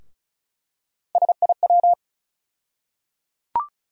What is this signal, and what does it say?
Morse code, heard as a keyed tone, about 36 wpm, 700 Hz. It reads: SIW
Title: USA Call Sign Postfixes - 3 Letters 35wpm